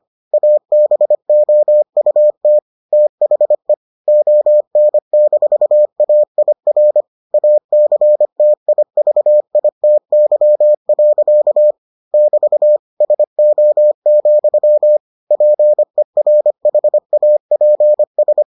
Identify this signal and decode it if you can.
Morse code, heard as a keyed tone, about 25 wpm, 600 Hz. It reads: ABOUT THE ON-AIR ACTIVITY. <BT> SO, PERHAPS